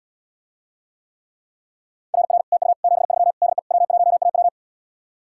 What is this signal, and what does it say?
Morse code, heard as a keyed tone, about 77 wpm, 700 Hz. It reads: RR TU 73 DE W1AW